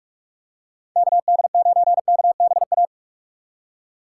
Morse code, heard as a keyed tone, about 41 wpm, 700 Hz. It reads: KD9KBA
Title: USA Call Signs - 6 Characters - 2x Character Spacing - 45wpm